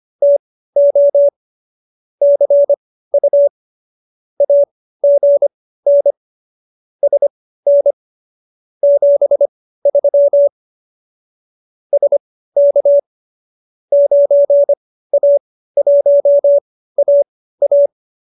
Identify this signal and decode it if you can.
Morse code, heard as a keyed tone, about 25 wpm, spaced as for 15 wpm, 570 Hz. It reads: TO CU AGN SN 73 SK 9A1AA